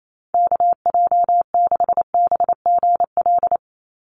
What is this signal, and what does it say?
Morse code, heard as a keyed tone, about 28 wpm, 700 Hz. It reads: KJ6BGL
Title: USA Call Signs 28wpm